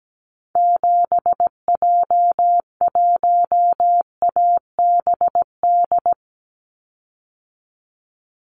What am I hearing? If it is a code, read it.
Morse code, heard as a keyed tone, about 17 wpm, 700 Hz. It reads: 7J1ABD
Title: Call Signs - 17wpm